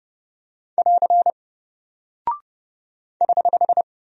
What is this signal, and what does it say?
Morse code, heard as a keyed tone, about 30 wpm, 700 Hz. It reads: <AR> <HH>
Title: Single Character (Letter:Number:Punctuation:Prosign) Alt 30wpm